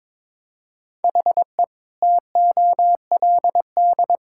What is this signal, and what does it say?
Morse code, heard as a keyed tone, about 22 wpm, 700 Hz. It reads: HE TOLD
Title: Sentences from Top 1600 Words 22wpm